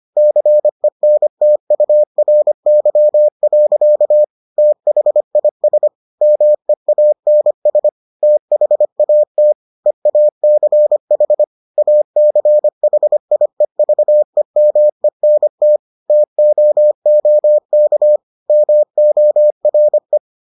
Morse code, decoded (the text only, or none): CENTURY. THIS MEANS THAT EACH ACHIEVEMENT TOOK MORE